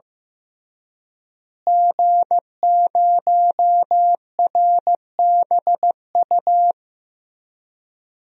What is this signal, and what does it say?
Morse code, heard as a keyed tone, about 15 wpm, 700 Hz. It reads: G0RBU
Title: DXCC-100 Call Signs - 5 Characters - Repeated 3x - 15wpm